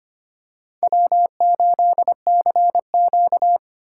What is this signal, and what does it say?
Morse code, heard as a keyed tone, about 25 wpm, 700 Hz. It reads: W8CQ
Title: USA Callsign DE Callsign - 4 Characters - 25wpm